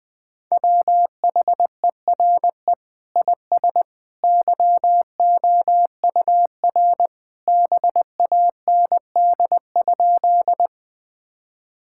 Morse code, heard as a keyed tone, about 21 wpm, 700 Hz. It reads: WHERE IS YOUR BAND?
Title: Sentences from Top 1600 Words 20wpm